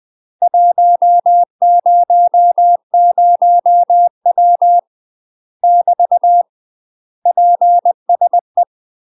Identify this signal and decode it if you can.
Morse code, heard as a keyed tone, about 20 wpm, 700 Hz. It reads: 100W <BT> PSE
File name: UK MASTER QSOs TRAINING / 20WPM / UK MASTER QSOs 0018